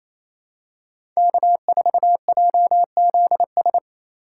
Morse code, heard as a keyed tone, about 28 wpm, 700 Hz. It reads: K4JZS
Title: USA Call Signs - 5 Characters - 2x Character Spacing - 28wpm